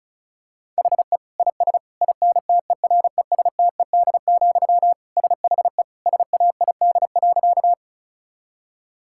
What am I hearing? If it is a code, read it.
Morse code, heard as a keyed tone, about 36 wpm, 700 Hz. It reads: HE IS INTERESTED, SHE SAID.
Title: Sentences from Top 1000 Words - Review Part 6 - 35wpm